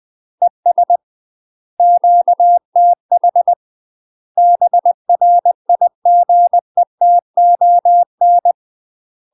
Morse code, heard as a keyed tone, about 20 wpm, 700 Hz. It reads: ES QTH BRIGETON
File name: UK MASTER QSOs TRAINING / 20WPM / UK MASTER QSOs 0040